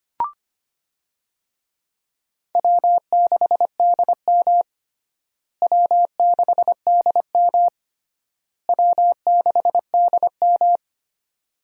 Morse code, heard as a keyed tone, about 25 wpm, 700 Hz. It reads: W6DM W6DM W6DM
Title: USA Call Signs - 4 Characters - Repeated 3x - 25wpm